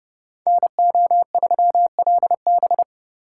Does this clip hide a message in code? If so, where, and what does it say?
Yes: Morse code from 0.5 to 2.8 s, heard as a keyed tone, about 29 wpm, 700 Hz. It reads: NO3LB